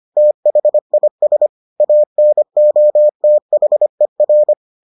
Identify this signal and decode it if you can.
Morse code, heard as a keyed tone, about 25 wpm, 600 Hz. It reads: THIS ANOTHER